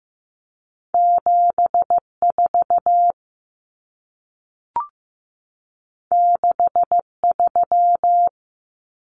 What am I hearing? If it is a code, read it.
Morse code, heard as a keyed tone, about 15 wpm, 700 Hz. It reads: 74 63
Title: Common Cut Numbers - 2-Digit 15wpm